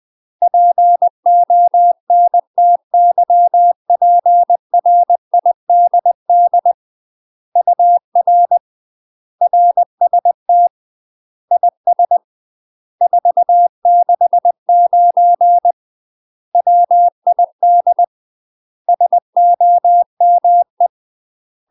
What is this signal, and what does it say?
Morse code, heard as a keyed tone, about 20 wpm, 700 Hz. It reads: PONTYPRIDD UR RST IS 469 WID SOME
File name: UK MASTER QSOs TRAINING/20WPM